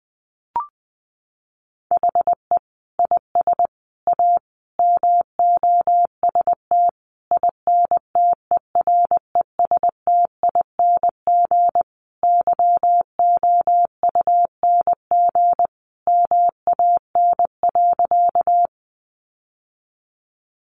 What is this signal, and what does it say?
Morse code, heard as a keyed tone, about 20 wpm, 700 Hz. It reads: HE IS A MOST INTERESTING YOUNG MAN.
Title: Sentences from Top 1000 Words 20wpm